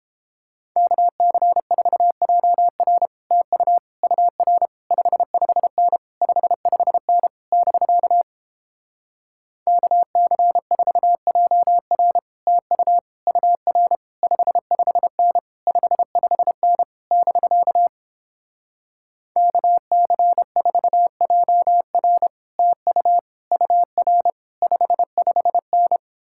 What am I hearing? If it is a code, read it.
Morse code, heard as a keyed tone, about 29 wpm, 700 Hz. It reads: KC4JR TU UR 55N 55N <BK> KC4JR TU UR 55N 55N <BK> KC4JR TU UR 55N